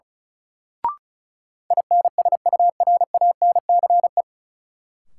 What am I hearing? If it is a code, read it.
Morse code, heard as a keyed tone, about 36 wpm, 700 Hz. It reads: INSURANCE
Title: Sentences from Top 1100 Words - New Word Drill - 35wpm